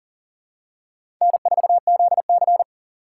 Morse code, heard as a keyed tone, about 40 wpm, 700 Hz. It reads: N4ZC
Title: USA Call Signs - 2x Character Spacing - 40wpm